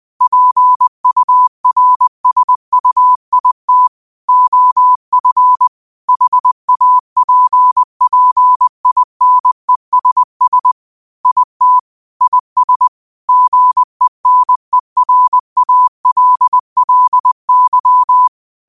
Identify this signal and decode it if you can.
Morse code, heard as a keyed tone, about 20 wpm, 1000 Hz. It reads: PURSUIT OF HAPPINESS IT IS GENERALLY